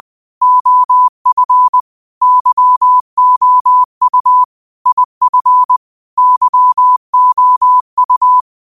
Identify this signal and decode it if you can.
Morse code, heard as a keyed tone, about 20 wpm, 1000 Hz. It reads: OF YOU IF YOU
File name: qotdfiles / 20wpm